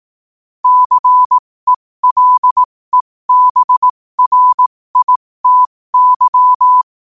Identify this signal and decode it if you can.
Morse code, heard as a keyed tone, about 18 wpm, spaced as for 15 wpm, 1000 Hz. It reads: CELEBRITY
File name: qotdfiles / 15wpm